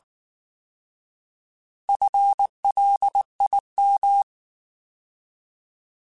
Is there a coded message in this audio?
Yes: Morse code from 1.9 to 4.2 s, heard as a keyed tone, about 19 wpm, 800 Hz. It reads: FLIM